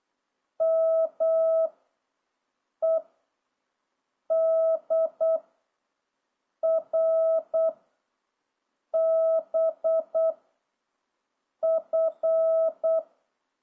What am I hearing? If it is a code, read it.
Morse code, heard as a keyed tone, about 8 wpm, 660 Hz. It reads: M E D R B F